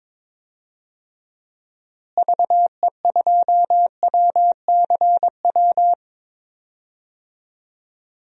Morse code, heard as a keyed tone, about 22 wpm, 700 Hz. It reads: VE2WCW
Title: DXCC-100 Call Signs - 6 Characters - Repeated 3x - 22wpm